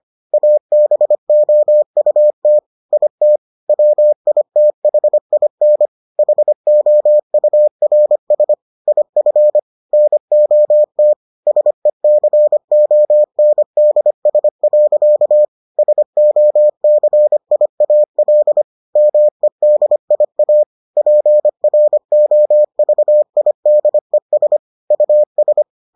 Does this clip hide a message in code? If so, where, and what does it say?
Yes: Morse code from 0.3 to 25.6 s, heard as a keyed tone, about 25 wpm, 600 Hz. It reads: ABOUT IT WITHIN HOURS IF NOT SECONDS. SOCIAL MEDIA PROVIDES US